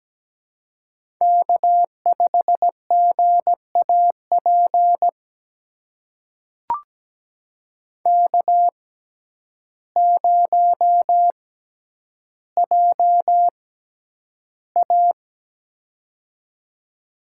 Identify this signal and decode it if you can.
Morse code, heard as a keyed tone, about 17 wpm, spaced as for 9 wpm, 700 Hz. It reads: K5GAP K0JA
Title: USA Call Signs - 6x Character Spacing - 17wpm